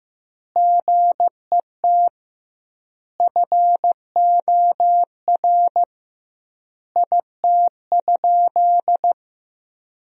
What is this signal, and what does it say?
Morse code, heard as a keyed tone, about 16 wpm, 700 Hz. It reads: GET FOR IT?
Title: 2x Word-Spacing - Sentences from Top 400 Words - 15wpm